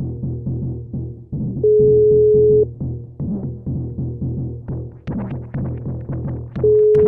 Ringback tone (Europe, 425 Hz).